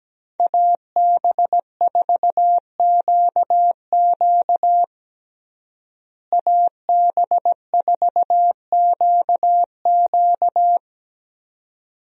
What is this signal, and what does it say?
Morse code, heard as a keyed tone, about 17 wpm, 700 Hz. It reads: AB4QQ AB4QQ
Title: USA Call Signs - 5 Characters - Repeated 2x - 17wpm